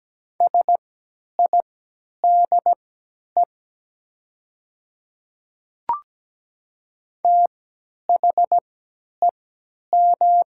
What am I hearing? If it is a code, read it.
Morse code, heard as a keyed tone, about 17 wpm, spaced as for 10 wpm, 700 Hz. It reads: SIDE THEM
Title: 3x Character Spacing - Top 300 Words - 17wpm